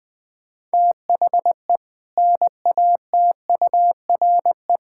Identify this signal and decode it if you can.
Morse code, heard as a keyed tone, about 19 wpm, 700 Hz. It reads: THE NATURE
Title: Sentences with 3-Word Phrases - Spoken Beginning - 20wpm